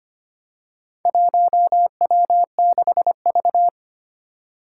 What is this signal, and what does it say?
Morse code, heard as a keyed tone, about 25 wpm, 700 Hz. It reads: 1W6V
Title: Four Challenging Letter-Number Combinations 25wpm